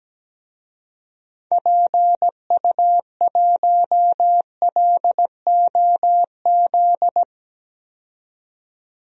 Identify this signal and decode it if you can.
Morse code, heard as a keyed tone, about 17 wpm, 700 Hz. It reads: PU1LOZ
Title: DXCC-100 Call Signs - 6 Characters - 17wpm